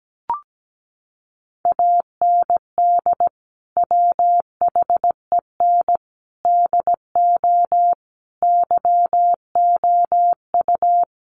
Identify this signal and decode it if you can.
Morse code, heard as a keyed tone, about 17 wpm, 700 Hz. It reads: AND WHEN DO YOU
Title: Sentences from Top 1100 Words 17wpm